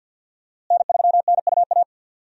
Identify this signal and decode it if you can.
Morse code, heard as a keyed tone, about 51 wpm, 700 Hz. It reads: N3NUA
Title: USA Call Signs - 5 Characters - Repeated 3x - 50wpm